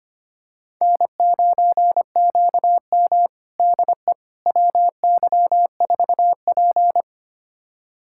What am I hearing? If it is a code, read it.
Morse code, heard as a keyed tone, about 25 wpm, 700 Hz. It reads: N9QM DE WY4P
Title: USA Callsign DE Callsign - 4 Characters - 25wpm